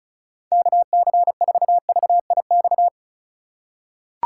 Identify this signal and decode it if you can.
Morse code, heard as a keyed tone, about 36 wpm, 690 Hz. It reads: KC4VIX
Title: USA Call Signs - 6 Characters - 3x Character Spacing - 35wpm